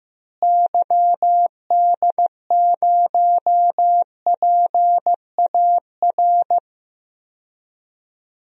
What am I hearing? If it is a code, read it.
Morse code, heard as a keyed tone, about 15 wpm, 700 Hz. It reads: YD0PAR
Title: DXCC-100 Call Signs - 6 Characters - Repeated 2x - 15wpm